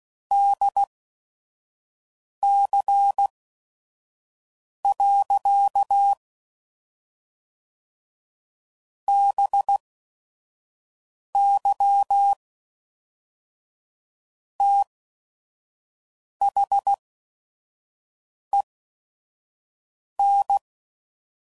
Morse code, heard as a keyed tone, about 16 wpm, spaced as for 5 wpm, 800 Hz. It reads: DC. BY THEN
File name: CW Code Groups @5WPM